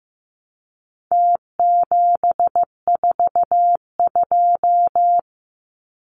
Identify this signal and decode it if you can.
Morse code, heard as a keyed tone, about 15 wpm, 700 Hz. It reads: T742
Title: Numbers - Contest Style - 4-Digit 15wpm